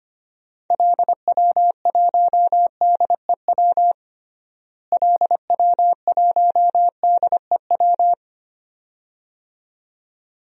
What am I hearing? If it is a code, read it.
Morse code, heard as a keyed tone, about 25 wpm, 700 Hz. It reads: LW1DEW LW1DEW
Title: DXCC-100 Call Signs - 6 Characters - Repeated 2x - 25wpm